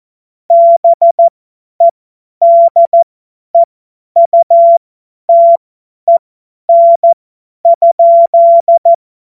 Morse code, heard as a keyed tone, about 14 wpm, spaced as for 10 wpm, 680 Hz. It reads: BEDEUTEN?
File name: Morsekurs / Witze / Witz 19. April 2022 50 F70